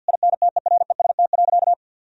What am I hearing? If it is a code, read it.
Morse code, heard as a keyed tone, about 50 wpm, 700 Hz. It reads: INTEREST.
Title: Sentences from Top 1000 Words - Review Part 4 - 50wpm